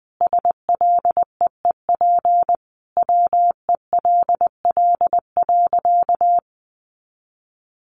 Morse code, heard as a keyed tone, about 20 wpm, 700 Hz. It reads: SLEEP WELL.
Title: Sentences from Top 500 Words 20wpm